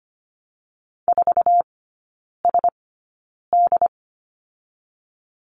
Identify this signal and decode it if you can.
Morse code, heard as a keyed tone, about 25 wpm, spaced as for 9 wpm, 700 Hz. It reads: 4SD
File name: Lesson 17e - Letter u - Three Characters 25wpm